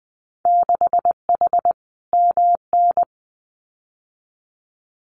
Morse code, heard as a keyed tone, about 20 wpm, 700 Hz. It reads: 6H MN
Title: Winter Field Day Exchanges Encore 20wpm